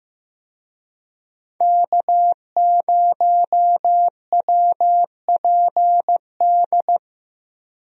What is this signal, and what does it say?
Morse code, heard as a keyed tone, about 15 wpm, 700 Hz. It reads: K0WPD
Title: USA Call Signs - 5 Characters - Repeated 2x - 15wpm